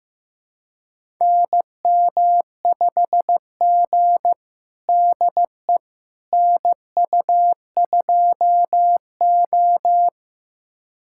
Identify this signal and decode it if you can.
Morse code, heard as a keyed tone, about 15 wpm, 700 Hz. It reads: NM5G DE NU2O
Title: USA Callsign DE Callsign - 4 Characters - 15wpm